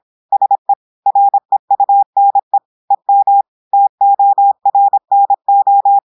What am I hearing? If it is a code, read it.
Morse code, heard as a keyed tone, about 26 wpm, 800 Hz. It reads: SE REUNE EM TORNO
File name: audios / portugais / portugais texte en clair 1 vitesse 26 mots minutes